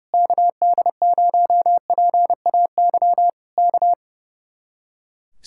Morse code, heard as a keyed tone, about 29 wpm, 700 Hz. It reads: KD0PAY K
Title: Speed-Racing Common QSO Phrases - 30wpm